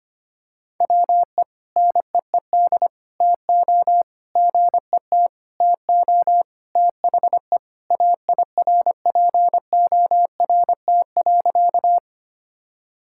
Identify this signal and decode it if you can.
Morse code, heard as a keyed tone, about 25 wpm, 700 Hz. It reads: WE NEED TO GET TO THE AIRPORT.
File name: practice / 25wpm / Sentences with 3-Word Phrases - 25wpm